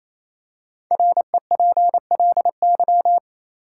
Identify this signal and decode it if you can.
Morse code, heard as a keyed tone, about 28 wpm, 700 Hz. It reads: REPLY
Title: Five Letter Words - 28wpm